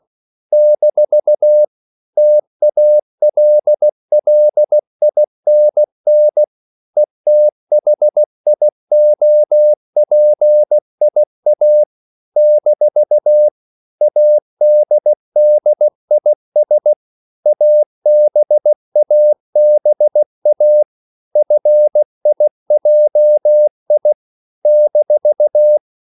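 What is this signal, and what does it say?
Morse code, heard as a keyed tone, about 16 wpm, 600 Hz. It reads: - TALLINN ETHIOPIA - ADDIS ABABA FIJI -
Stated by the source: CW practice of national capitals at 16 words per minute, from ON6ZQ.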